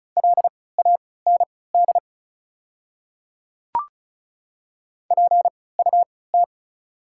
Morse code, heard as a keyed tone, about 36 wpm, spaced as for 20 wpm, 690 Hz. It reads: LAND PUT
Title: 3x Character Spacing - Top 500 Words - 35wpm